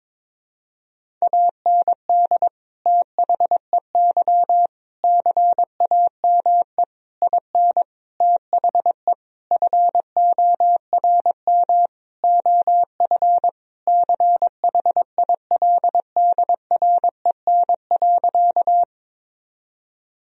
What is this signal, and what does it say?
Morse code, heard as a keyed tone, about 21 wpm, 700 Hz. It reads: AND THEY CAME IN THE FORM OF CHILDREN.
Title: Sentences from Top 1000 Words - Review Part 1 - 22wpm